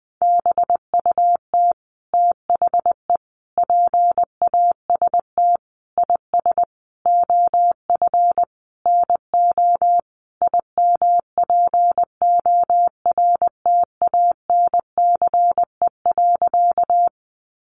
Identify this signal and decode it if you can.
Morse code, heard as a keyed tone, about 20 wpm, 700 Hz. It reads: BUT THE PAST IS OF NO IMPORTANCE.
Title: Sentences from Top 1200 Words 20wpm